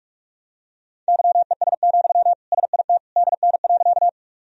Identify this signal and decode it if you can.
Morse code, heard as a keyed tone, about 46 wpm, 700 Hz. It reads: YES, SIT DN.
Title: Sentences from Top 500 Words and Abbreviations 45wpm